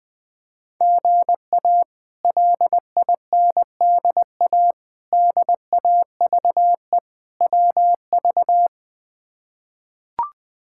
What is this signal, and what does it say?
Morse code, heard as a keyed tone, about 20 wpm, 700 Hz. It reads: GA LINDA DAVE WV
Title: K1USN SST QSO Practice - 20wpm